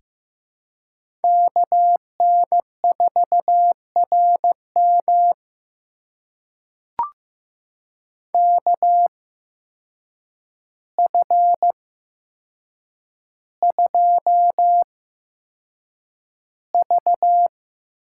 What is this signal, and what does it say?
Morse code, heard as a keyed tone, about 15 wpm, spaced as for 7 wpm, 700 Hz. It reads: KN4RM KF2V
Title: USA Call Signs - 8x Character Spacing - 15wpm